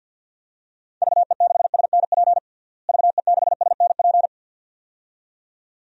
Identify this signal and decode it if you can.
Morse code, heard as a keyed tone, about 51 wpm, 700 Hz. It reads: VE6SNP VE6SNP
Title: DXCC-100 Call Signs - 6 Characters - Repeated 2x - 50wpm